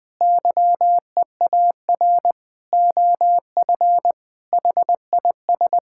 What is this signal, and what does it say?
Morse code, heard as a keyed tone, about 20 wpm, 700 Hz. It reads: YEAR OF HIS
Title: Sentences with 8-Word Phrases - 20wpm